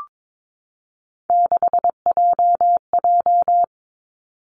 Morse code, heard as a keyed tone, about 22 wpm, 700 Hz. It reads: 6JJ
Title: Three Challenging Letter-Number Combinations 22wpm